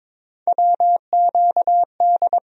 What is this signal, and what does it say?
Morse code, heard as a keyed tone, about 21 wpm, 700 Hz. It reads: WQD
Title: USA Call Sign Postfixes - 3 Letters 22wpm